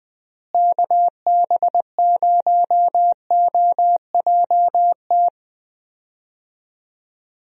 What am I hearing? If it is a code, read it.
Morse code, heard as a keyed tone, about 20 wpm, 700 Hz. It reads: KB0OJT
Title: USA Call Signs - 6 Characters - Repeated 2x - 20wpm